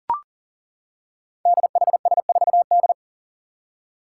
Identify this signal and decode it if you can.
Morse code, heard as a keyed tone, about 40 wpm, 700 Hz. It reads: DHS4D
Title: Five Challenging Letter-Number Combinations 40wpm